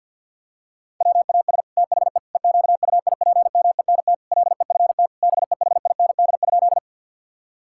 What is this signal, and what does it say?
Morse code, heard as a keyed tone, about 51 wpm, 700 Hz. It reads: WAS THE EQUIPMENT LEFT BEHIND?
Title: Sentences from Top 1000 Words - Review Part 5 - 50wpm